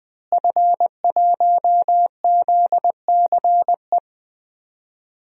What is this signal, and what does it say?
Morse code, heard as a keyed tone, about 20 wpm, 700 Hz. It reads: F1ZCE
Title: DXCC-100 Call Signs - 5 Characters - Repeated 2x - 20wpm